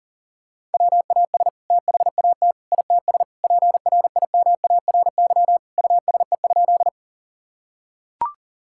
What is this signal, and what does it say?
Morse code, heard as a keyed tone, about 40 wpm, 700 Hz. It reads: WAS THAT ITS PRIMARY USE?
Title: Sentences from Top 1000 Words - Review Part 5 - 40wpm